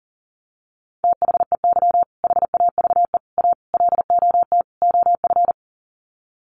Morse code, heard as a keyed tone, about 40 wpm, 700 Hz. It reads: THEY HAVE A LOT OF
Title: Sets of 5 Words Encore2 40wpm